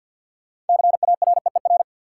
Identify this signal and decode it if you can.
Morse code, heard as a keyed tone, about 52 wpm, 700 Hz. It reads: CAREER
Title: Sentences from Top 1000 Words - New Word Drill - 50wpm